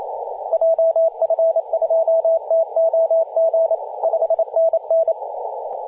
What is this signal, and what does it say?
Morse code, heard as a keyed tone, about 28 wpm, 660 Hz, with radio noise. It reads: JF2TOG 5NN
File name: vk9ln-14-005mhz-cw